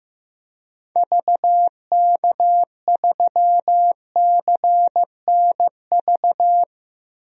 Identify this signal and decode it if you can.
Morse code, heard as a keyed tone, about 15 wpm, 700 Hz. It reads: VK3CNV
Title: DXCC-100 Call Signs - 6 Characters - 15wpm